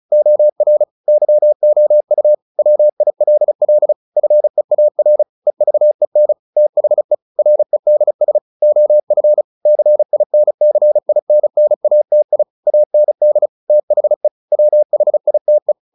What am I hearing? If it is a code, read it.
Morse code, heard as a keyed tone, about 34 wpm, 600 Hz. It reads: OR YOU WILL FEAR EVEN THE REDS OF CINCINNATI AND THE WHITE
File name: the old man and the sea / the old man and the sea 1 35wpm